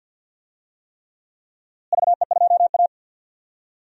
Morse code, heard as a keyed tone, about 52 wpm, 700 Hz. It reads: VE2A
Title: DXCC-100 Call Signs - 4 Characters - Repeated 3x - 50wpm